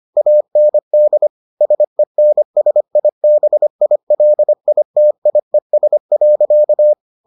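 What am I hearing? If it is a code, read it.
Morse code, heard as a keyed tone, about 25 wpm, 600 Hz. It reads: AND SENSIBILITIES.
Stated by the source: Morse edition of the Foundations of Amateur Radio episodes.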